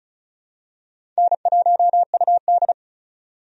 Morse code, heard as a keyed tone, about 34 wpm, 700 Hz. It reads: N1UD
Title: USA Call Signs - 4 Characters - 35wpm